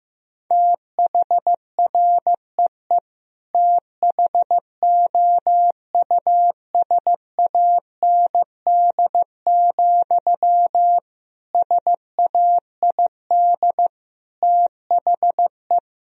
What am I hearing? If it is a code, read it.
Morse code, heard as a keyed tone, about 16 wpm, 700 Hz. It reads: THREE THOUSAND, SAID THE
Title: Sentences from Top 1000 Words - Review Part 3 - 15wpm